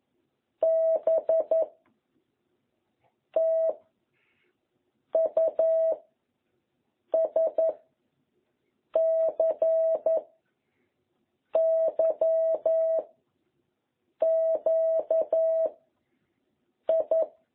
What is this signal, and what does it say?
Morse code, heard as a keyed tone, about 11 wpm, 660 Hz. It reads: B T U S C Y Q I